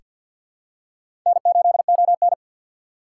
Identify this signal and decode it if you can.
Morse code, heard as a keyed tone, about 47 wpm, 700 Hz. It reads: N8ON
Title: USA Call Signs - 6x Character Spacing - 50wpm